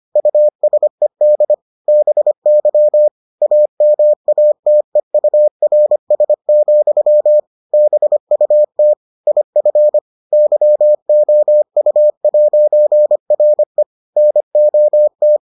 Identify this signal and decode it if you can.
Morse code, heard as a keyed tone, about 25 wpm, 600 Hz. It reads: USED BY AMATEURS, BUT IF YOU'RE NOT